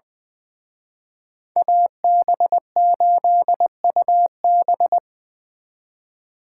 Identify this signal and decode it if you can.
Morse code, heard as a keyed tone, about 24 wpm, 700 Hz. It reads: AB8UB
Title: Speed Racing - USA Call Signs 20wpm